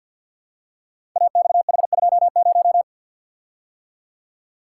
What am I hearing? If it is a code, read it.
Morse code, heard as a keyed tone, about 48 wpm, 700 Hz. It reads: AXH20
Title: Code Groups - 5 Letter:Numbers - 50wpm